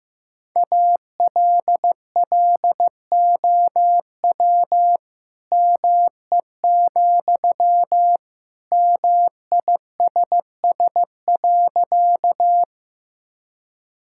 Morse code, heard as a keyed tone, about 15 wpm, 690 Hz. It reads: ALLOW ME, MISS.
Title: Sentences from Top 1000 Words - Review Part 5 - 15wpm